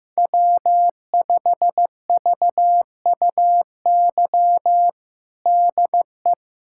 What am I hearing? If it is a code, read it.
Morse code, heard as a keyed tone, about 15 wpm, 700 Hz. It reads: W5VUY DE
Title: USA Callsign DE Callsign - 5 Characters - 15wpm